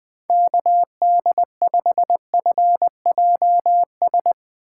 Morse code, heard as a keyed tone, about 20 wpm, 700 Hz. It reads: KD5FJS
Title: Speed Racing - USA 2x3 Call Signs 17wpm